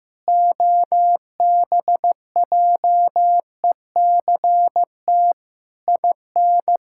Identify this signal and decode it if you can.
Morse code, heard as a keyed tone, about 15 wpm, 700 Hz. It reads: OBJECT IN
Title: Sentences from Top 1000 Words - Review Part 6 - 15wpm